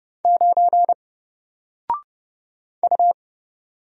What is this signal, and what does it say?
Morse code, heard as a keyed tone, about 30 wpm, 700 Hz. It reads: 9 U
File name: Single-Letter-Number-30wpm